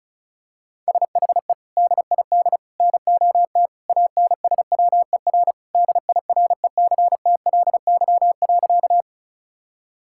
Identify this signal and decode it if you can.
Morse code, heard as a keyed tone, about 36 wpm, 700 Hz. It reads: SHE DID NOT ANSWER DIRECTLY.
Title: Sentences from Top 1000 Words - Review Part 5 - 35wpm